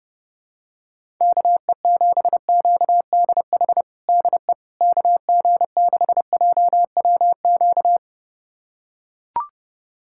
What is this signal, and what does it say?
Morse code, heard as a keyed tone, about 32 wpm, 700 Hz. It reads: KE7QDH DE KG6JWQ
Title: USA Callsign DE Callsign - 6 Characters - 30wpm